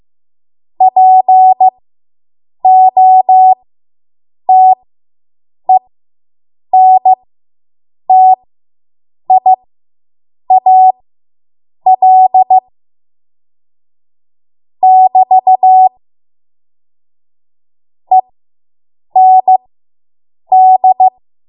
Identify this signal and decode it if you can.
Morse code, heard as a keyed tone, about 15 wpm, spaced as for 7 wpm, 750 Hz. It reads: POTENTIAL <BT> END